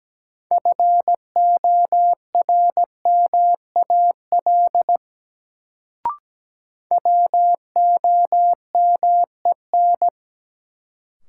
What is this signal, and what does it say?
Morse code, heard as a keyed tone, about 17 wpm, 700 Hz. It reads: FORMAL WOMEN
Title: Sentences from Top 1300 Words - New Word Drill - 17wpm